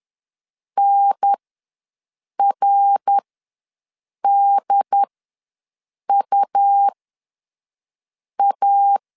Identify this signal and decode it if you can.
Morse code, heard as a keyed tone, about 10 wpm, spaced as for 5 wpm, 800 Hz. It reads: NRDUA
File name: LEZIONI / 5 LBVC / 5 L ws 10wpm 3636